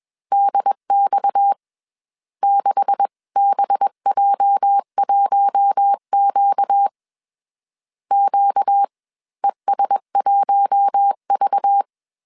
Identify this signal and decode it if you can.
Morse code, heard as a keyed tone, about 21 wpm, spaced as for 13 wpm, 800 Hz. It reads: DX6BJ 1QQES 1V